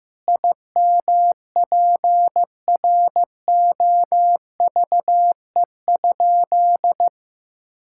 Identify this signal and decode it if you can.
Morse code, heard as a keyed tone, about 16 wpm, 700 Hz. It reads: IMPROVE?
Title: Sentences from Top 1000 Words - Review Part 5 - 15wpm